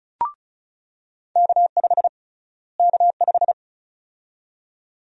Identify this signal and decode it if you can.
Morse code, heard as a keyed tone, about 36 wpm, 700 Hz. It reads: K5 K5
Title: USA Call Sign Prefixes - Repeated 2x - 35wpm